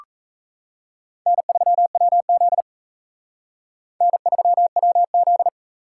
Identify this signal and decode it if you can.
Morse code, heard as a keyed tone, about 36 wpm, 700 Hz. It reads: N3WZ N3WZ